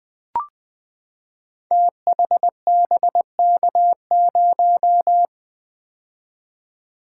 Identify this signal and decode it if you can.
Morse code, heard as a keyed tone, about 20 wpm, 700 Hz. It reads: THBK0
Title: Code Groups - 5 Letter:Numbers - 20wpm